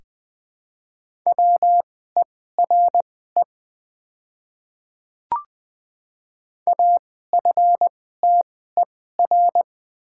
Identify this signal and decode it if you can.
Morse code, heard as a keyed tone, about 20 wpm, spaced as for 15 wpm, 700 Hz. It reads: WERE AFTER
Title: 2x Character Spacing - Top 200 Words - 20wpm